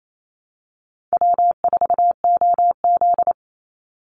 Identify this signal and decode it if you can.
Morse code, heard as a keyed tone, about 28 wpm, 700 Hz. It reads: W4OZ
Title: USA Call Signs 28wpm